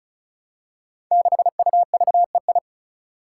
Morse code, heard as a keyed tone, about 36 wpm, 700 Hz. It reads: 6UVEI
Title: Five Challenging Letter-Number Combinations 35wpm